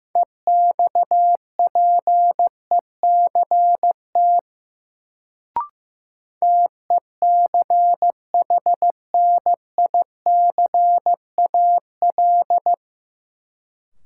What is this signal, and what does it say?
Morse code, heard as a keyed tone, about 15 wpm, 700 Hz. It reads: EXPECT TECHNICAL
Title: Sentences from Top 1100 Words - New Word Drill Alt - 15wpm